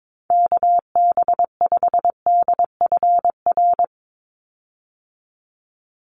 Morse code, heard as a keyed tone, about 22 wpm, 700 Hz. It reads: KB5DFR
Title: USA Call Signs 22wpm